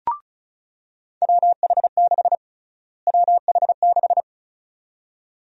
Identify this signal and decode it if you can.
Morse code, heard as a keyed tone, about 36 wpm, 700 Hz. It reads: WH6 WH6
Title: USA Call Sign Prefixes - Repeated 2x - 35wpm